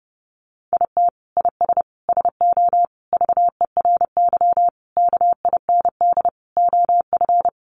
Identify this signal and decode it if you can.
Morse code, heard as a keyed tone, about 30 wpm, 700 Hz. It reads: IT IS SO VERY KIND OF
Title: Sentences from Top 300 Words 30wpm